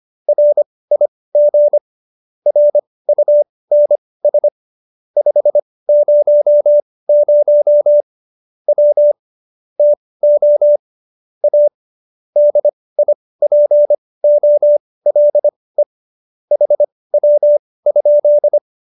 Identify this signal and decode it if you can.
Morse code, heard as a keyed tone, about 25 wpm, spaced as for 18 wpm, 590 Hz. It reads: RIG RUNS 500 W TO A DIPOLE HW?
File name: qso215b 18